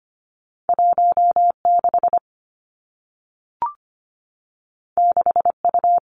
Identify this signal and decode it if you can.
Morse code, heard as a keyed tone, about 25 wpm, 700 Hz. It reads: 1 6 6 U
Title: Two Challenging Letter-Number Combinations 25wpm